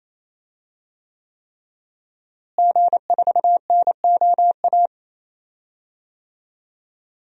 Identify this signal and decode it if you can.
Morse code, heard as a keyed tone, about 28 wpm, 700 Hz. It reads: G4NOA
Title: DXCC-100 Call Signs - 5 Characters - Repeated 3x - 28wpm